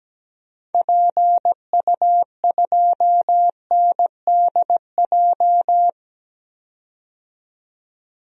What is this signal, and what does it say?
Morse code, heard as a keyed tone, about 17 wpm, 700 Hz. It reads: PU2NDJ
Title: DXCC-100 Call Signs - 6 Characters - 17wpm